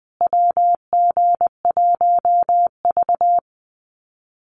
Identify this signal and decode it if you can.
Morse code, heard as a keyed tone, about 20 wpm, 700 Hz. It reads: WG1V
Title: USA Call Signs Repeated 3x 20wpm